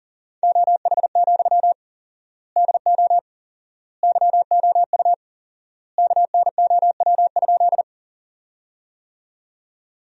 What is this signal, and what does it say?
Morse code, heard as a keyed tone, about 40 wpm, 700 Hz. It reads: OH, DO YOU KNOW?
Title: 4x Word-Spacing - Sentences from Top 500 Words - 40wpm